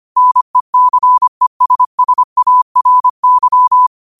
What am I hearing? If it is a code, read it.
Morse code, heard as a keyed tone, about 25 wpm, 1000 Hz. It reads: NECESSARY
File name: qotdfiles / 25wpm